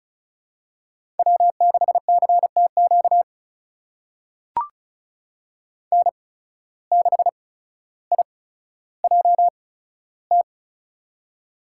Morse code, heard as a keyed tone, about 36 wpm, spaced as for 15 wpm, 700 Hz. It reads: W6CTQ N6IJT
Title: USA Call Signs - 5 Characters - 8x Character Spacing - 35wpm